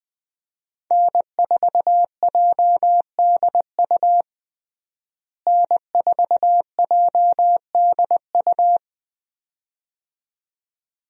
Morse code, heard as a keyed tone, about 21 wpm, 700 Hz. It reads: N4JDU N4JDU
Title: USA Call Signs - 5 Characters - Repeated 2x - 20wpm